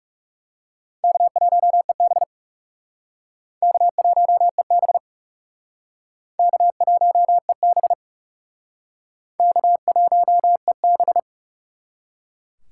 Morse code, heard as a keyed tone, about 39 wpm, 700 Hz. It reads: K1EB K1EB K1EB K1EB